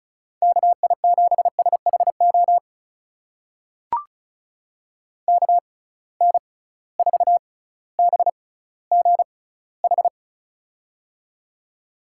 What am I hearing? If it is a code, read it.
Morse code, heard as a keyed tone, about 36 wpm, spaced as for 18 wpm, 700 Hz. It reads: KI7SHO KN4BGH
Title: USA Call Signs - 6 Characters - 6x Character Spacing - 35wpm